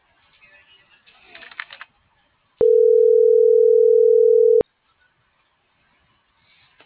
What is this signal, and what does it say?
Ringback tone (North America), 440 + 480 Hz.